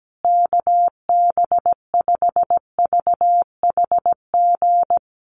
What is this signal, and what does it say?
Morse code, heard as a keyed tone, about 17 wpm, 700 Hz. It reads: KB5VHG
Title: USA Call Signs Repeated 3x 17wpm